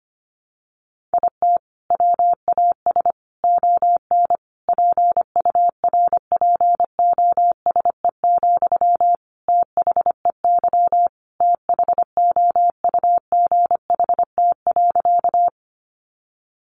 Morse code, heard as a keyed tone, about 25 wpm, 700 Hz. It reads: IT WAS ON PURPOSE, THEY THOUGHT.
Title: Sentences from Top 800 Words 25wpm